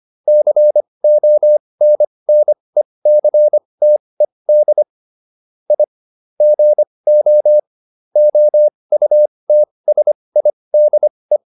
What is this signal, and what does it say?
Morse code, heard as a keyed tone, about 25 wpm, spaced as for 20 wpm, 600 Hz. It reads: CONNECTED I GO OUTSIDE